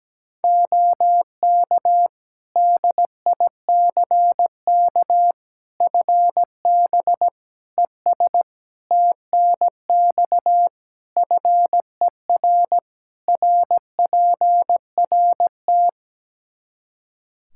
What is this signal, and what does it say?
Morse code, heard as a keyed tone, about 17 wpm, 700 Hz. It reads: OK DICK FB ES TNX FER RPRT
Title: Practice QSO - Exchange 2 - Info - 17wpm